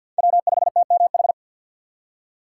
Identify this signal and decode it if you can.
Morse code, heard as a keyed tone, about 64 wpm, 700 Hz. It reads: W5TMH